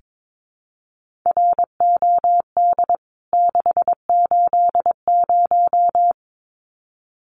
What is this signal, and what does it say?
Morse code, heard as a keyed tone, about 22 wpm, 700 Hz. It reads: ROD 680
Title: CWT Exchange 22wpm